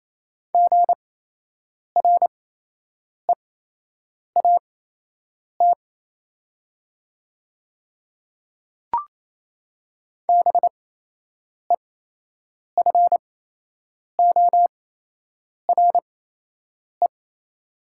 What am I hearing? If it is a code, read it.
Morse code, heard as a keyed tone, about 28 wpm, spaced as for 8 wpm, 700 Hz. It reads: GREAT BEFORE
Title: 8x Character Spacing - Top 500 Words - 28wpm